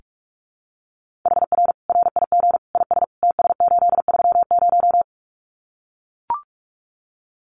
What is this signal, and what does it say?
Morse code, heard as a keyed tone, about 44 wpm, 700 Hz. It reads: HR RIG IS TS830
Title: Common QSO Phrases 45wpm